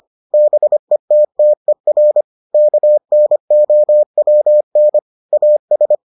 Morse code, heard as a keyed tone, about 25 wpm, 600 Hz. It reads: BETTER KNOWN AS